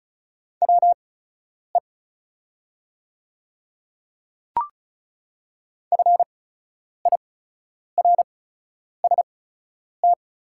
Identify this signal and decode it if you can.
Morse code, heard as a keyed tone, about 36 wpm, spaced as for 10 wpm, 690 Hz. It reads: WE FIRST
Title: 8x Character Spacing - Top 200 Words - 35wpm